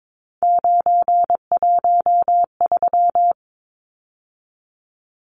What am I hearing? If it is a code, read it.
Morse code, heard as a keyed tone, about 22 wpm, 700 Hz. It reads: 913
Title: Numbers - 3-Digit 22wpm